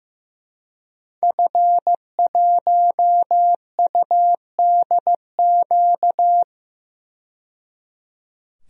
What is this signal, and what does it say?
Morse code, heard as a keyed tone, about 15 wpm, 700 Hz. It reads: F1UDQ
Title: DXCC-100 Call Signs - 5 Characters - 15wpm